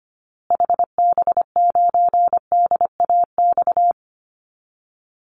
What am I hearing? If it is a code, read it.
Morse code, heard as a keyed tone, about 25 wpm, 700 Hz. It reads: HB9DAX
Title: Call Signs Repeated 2x - 25wpm